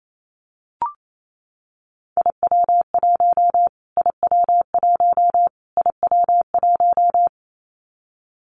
Morse code, heard as a keyed tone, about 28 wpm, 700 Hz. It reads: IW1 IW1 IW1
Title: Callsign Prefixes for Top 25 DXCC - Repeated 3x - 28wpm